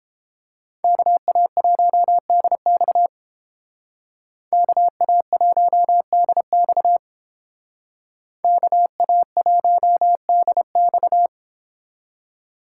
Morse code, heard as a keyed tone, about 31 wpm, 700 Hz. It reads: KA1DX KA1DX KA1DX